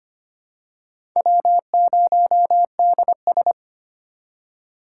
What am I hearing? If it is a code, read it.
Morse code, heard as a keyed tone, about 25 wpm, 700 Hz. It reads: W0DS
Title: USA Call Signs - 2x Character Spacing - 25wpm